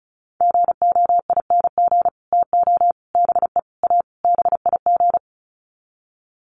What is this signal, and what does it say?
Morse code, heard as a keyed tone, about 34 wpm, 700 Hz. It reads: GOING TO BE A BIG
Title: Sets of 5 Words Encore2 35wpm